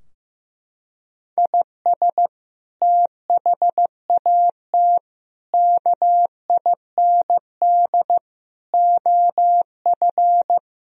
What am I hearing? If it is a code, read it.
Morse code, heard as a keyed tone, about 16 wpm, 700 Hz. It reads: IS THAT KIND OF
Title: Sentences from Top 1000 Words - Review Part 3 - 15wpm